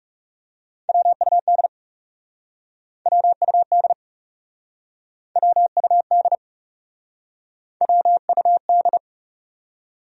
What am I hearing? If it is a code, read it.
Morse code, heard as a keyed tone, about 40 wpm, 700 Hz. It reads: WUD WUD WUD WUD